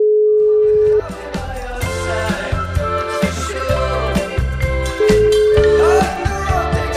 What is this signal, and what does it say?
Ringback tone (Europe, 425 Hz), with noise.